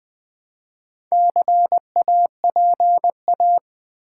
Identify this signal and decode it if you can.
Morse code, heard as a keyed tone, about 20 wpm, 700 Hz. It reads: CAPA
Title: Build A Word - 4 Letters At A Time - 10-Letter Words - 20wpm